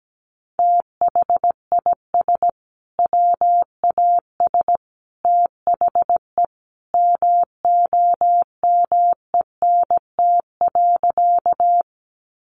Morse code, heard as a keyed tone, about 17 wpm, 700 Hz. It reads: THIS WAS THE MOMENT.
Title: Sentences from Top 1000 Words 17wpm